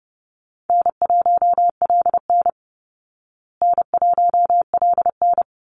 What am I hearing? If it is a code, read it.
Morse code, heard as a keyed tone, about 30 wpm, 700 Hz. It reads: N1LN N1LN
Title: Easy Call Signs Repeated 2x - 30wpm